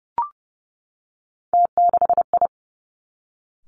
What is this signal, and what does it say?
Morse code, heard as a keyed tone, about 30 wpm, 700 Hz. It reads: T6I
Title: Three Challenging Letter-Number Combinations 30wpm